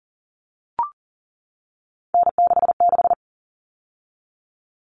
Morse code, heard as a keyed tone, about 40 wpm, 700 Hz. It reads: N66
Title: All Cut Numbers - 3-Digit 40wpm